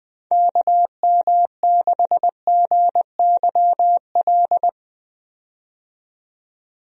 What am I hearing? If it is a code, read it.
Morse code, heard as a keyed tone, about 20 wpm, 700 Hz. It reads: KM6GYL
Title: USA Call Signs - 6 Characters - 20wpm